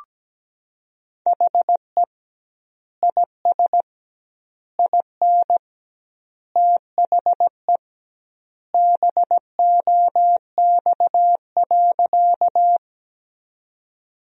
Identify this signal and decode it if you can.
Morse code, heard as a keyed tone, about 17 wpm, 700 Hz. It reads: HE IS IN THE BOX.
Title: 2x Word-Spacing - Sentences from Top 500 Words - 17wpm